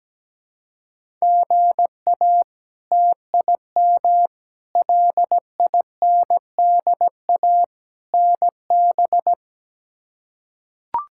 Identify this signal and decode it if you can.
Morse code, heard as a keyed tone, about 17 wpm, 700 Hz. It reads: GA TIM LINDA NB
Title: K1USN SST QSO Practice - 17wpm